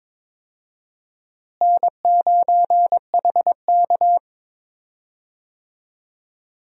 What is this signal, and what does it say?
Morse code, heard as a keyed tone, about 22 wpm, 700 Hz. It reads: N9HK
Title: USA Call Signs - 4 Characters - Repeated 3x - 22wpm